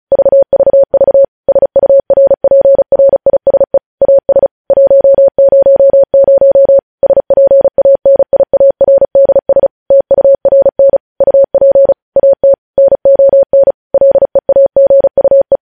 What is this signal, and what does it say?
Morse code, heard as a keyed tone, about 36 wpm, 550 Hz. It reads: VVV SURPRISE AS 100 SPANIARDS TURN UP AT NON LEAGUE
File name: podcast / News headlines-1x-35x35